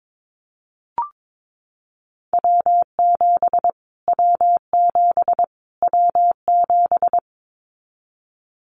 Morse code, heard as a keyed tone, about 22 wpm, 700 Hz. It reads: W7 W7 W7
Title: Callsign Prefixes for Top 25 DXCC - Repeated 3x - 22wpm